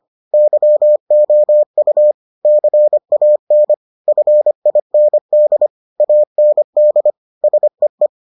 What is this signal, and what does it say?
Morse code, heard as a keyed tone, about 25 wpm, 600 Hz. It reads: YOU CAN FIND AND SEE